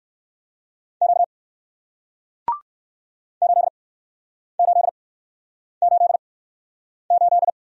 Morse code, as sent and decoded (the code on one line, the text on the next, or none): --.- / ---.. ---.. ---.. ---..
Q 8888